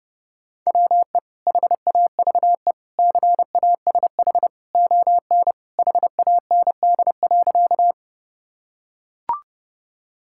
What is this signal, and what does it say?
Morse code, heard as a keyed tone, about 30 wpm, 700 Hz. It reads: WE HAVE CASH ON HAND.
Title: Sentences from Top 1000 Words - Review Part 5 - 30wpm